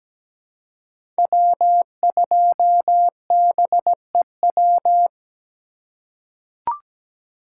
Morse code, heard as a keyed tone, about 17 wpm, 700 Hz. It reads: W2BEW
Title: USA Call Signs - 5 Characters - 2x Character Spacing - 17wpm